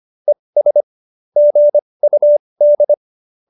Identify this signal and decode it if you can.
Morse code, heard as a keyed tone, about 25 wpm, spaced as for 20 wpm, 590 Hz. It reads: ES GUD